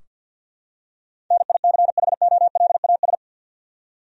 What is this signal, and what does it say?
Morse code, heard as a keyed tone, about 46 wpm, 700 Hz. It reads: NICHOLAS
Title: USA Common Names - 50wpm